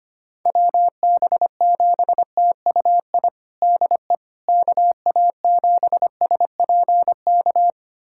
Morse code, heard as a keyed tone, about 25 wpm, 700 Hz. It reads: WB7TUI DE KA7SPK
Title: USA Callsign DE Callsign 25wpm